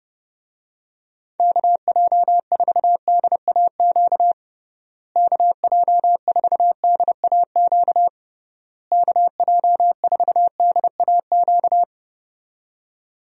Morse code, heard as a keyed tone, about 32 wpm, 700 Hz. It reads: KJ4DAQ KJ4DAQ KJ4DAQ
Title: USA Call Signs - 6 Characters - Repeated 3x - 30wpm